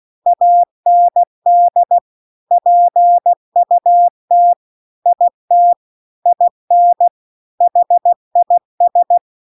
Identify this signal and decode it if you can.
Morse code, heard as a keyed tone, about 16 wpm, 700 Hz. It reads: AND PUT IT IN HIS